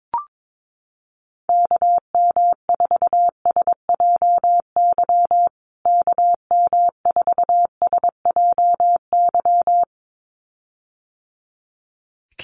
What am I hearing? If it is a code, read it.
Morse code, heard as a keyed tone, about 22 wpm, 700 Hz. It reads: KM4SJY KM4SJY
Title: USA Call Signs Repeated 2x 22wpm